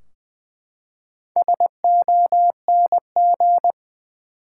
Morse code, heard as a keyed tone, about 20 wpm, 700 Hz. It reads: SONG
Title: Four Letter Words - 20wpm